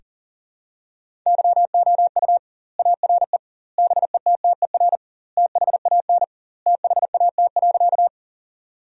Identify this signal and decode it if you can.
Morse code, heard as a keyed tone, about 37 wpm, 700 Hz. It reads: YOU ARE BETTER THAN THAT.
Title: 2x Word-Spacing - Sentences from Top 400 Words - 40wpm